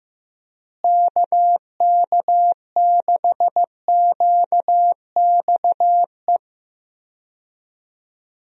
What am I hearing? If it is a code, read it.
Morse code, heard as a keyed tone, about 15 wpm, 700 Hz. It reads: KK6QXE
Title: USA Call Signs - 6 Characters - Repeated 2x - 15wpm